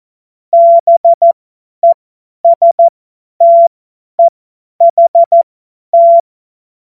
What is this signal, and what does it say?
Morse code, heard as a keyed tone, about 14 wpm, spaced as for 10 wpm, 680 Hz. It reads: BESTEHT
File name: Morsekurs / Witze / Witz 31. Mai 2022 50 F70